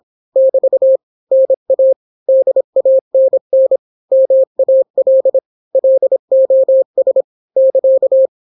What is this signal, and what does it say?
Morse code, heard as a keyed tone, about 26 wpm, 530 Hz. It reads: <BT> NA DANN MAL LOS <KA>